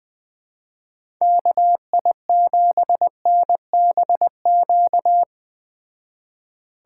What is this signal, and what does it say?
Morse code, heard as a keyed tone, about 20 wpm, 700 Hz. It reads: KI7NBQ